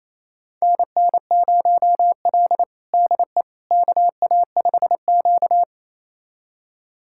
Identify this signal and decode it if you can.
Morse code, heard as a keyed tone, about 28 wpm, 700 Hz. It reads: NN0L DE KA5Q
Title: USA Callsign DE Callsign - 4 Characters - 28wpm